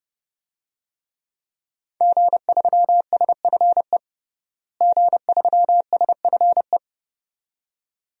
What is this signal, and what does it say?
Morse code, heard as a keyed tone, about 29 wpm, 700 Hz. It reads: G3SFE G3SFE
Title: DXCC-100 Call Signs - 5 Characters - Repeated 2x - 30wpm